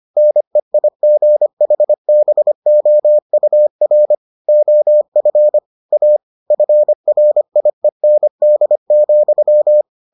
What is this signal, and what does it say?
Morse code, heard as a keyed tone, about 25 wpm, 600 Hz. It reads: NEIGHBOUR OF A FRIEND,